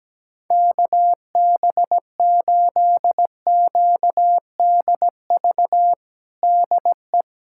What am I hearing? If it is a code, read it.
Morse code, heard as a keyed tone, about 17 wpm, 700 Hz. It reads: KB8QDV DE
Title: USA Callsign DE Callsign - 6 Characters - 17wpm